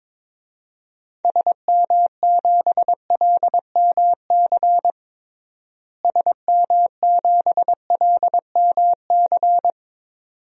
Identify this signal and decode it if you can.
Morse code, heard as a keyed tone, about 22 wpm, 700 Hz. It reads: SM7LMC SM7LMC
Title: DXCC-100 Call Signs - 6 Characters - Repeated 2x - 22wpm